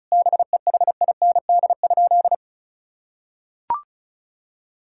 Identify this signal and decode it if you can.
Morse code, heard as a keyed tone, about 34 wpm, 700 Hz. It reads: BEHIND?
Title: Sentences from Top 1000 Words - Review Part 5 - 35wpm